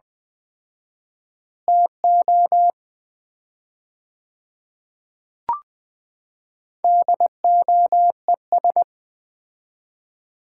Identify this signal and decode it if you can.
Morse code, heard as a keyed tone, about 20 wpm, 700 Hz. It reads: TO DOES
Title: 4x Word-Spacing - Sets of 2 Words - 20wpm